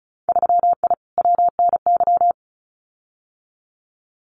Morse code, heard as a keyed tone, about 34 wpm, 700 Hz. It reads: 3I WNY
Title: Winter Field Day Exchanges Encore 35wpm